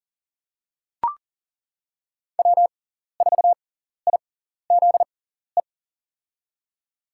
Morse code, heard as a keyed tone, about 42 wpm, spaced as for 14 wpm, 700 Hz. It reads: W4IZE
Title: USA Call Signs - 5 Characters - 6x Character Spacing - 40wpm